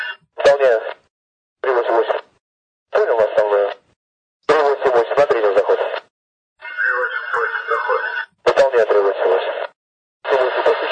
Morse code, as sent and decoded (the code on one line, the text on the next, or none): none